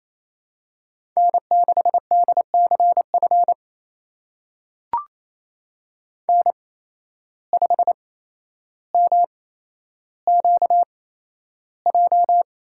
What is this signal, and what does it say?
Morse code, heard as a keyed tone, about 28 wpm, spaced as for 12 wpm, 700 Hz. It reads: N6DCF N5MQJ
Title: USA Call Signs - 5 Characters - 8x Character Spacing - 28wpm